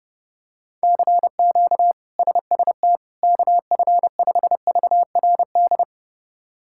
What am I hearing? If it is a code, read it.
Morse code, heard as a keyed tone, about 30 wpm, 700 Hz. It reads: CQ SST KF5VRD
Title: K1USN SST QSO Practice - 30wpm